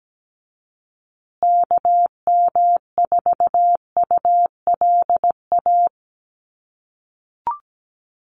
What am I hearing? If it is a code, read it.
Morse code, heard as a keyed tone, about 17 wpm, 700 Hz. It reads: KM4ULA
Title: USA Call Signs Repeated 2x 17wpm